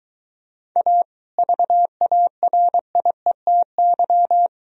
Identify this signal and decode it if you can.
Morse code, heard as a keyed tone, about 22 wpm, 700 Hz. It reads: A VARIETY